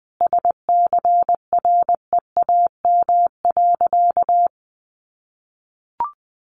Morse code, heard as a keyed tone, about 20 wpm, 700 Hz. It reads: SCREAM.
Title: A Sound Of Thunder 20wpm